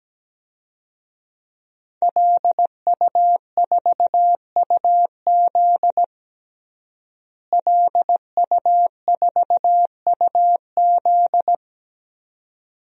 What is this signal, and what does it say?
Morse code, heard as a keyed tone, about 17 wpm, 700 Hz. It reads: LU4UZ LU4UZ
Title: DXCC-100 Call Signs - 5 Characters - Repeated 2x - 17wpm